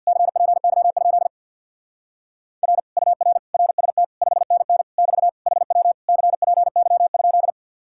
Morse code, heard as a keyed tone, about 58 wpm, 700 Hz. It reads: CPY? R UR RST 5NN <BT> HW CPY?
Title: Speed-Racing Common QSO Phrases - 50wpm